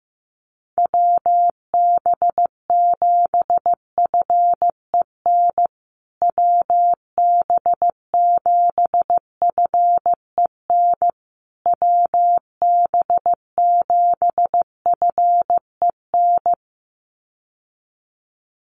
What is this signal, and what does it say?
Morse code, heard as a keyed tone, about 15 wpm, 700 Hz. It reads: WB7FEN WB7FEN WB7FEN
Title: USA Call Signs Repeated 3x 15wpm